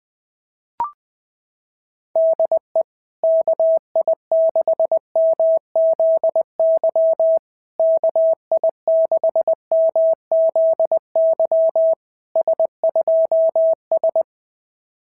Morse code, heard as a keyed tone, about 19 wpm, 650 Hz. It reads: DE KI6MZY KI6MZY S2S
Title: SOTA QSO Practice 20wpm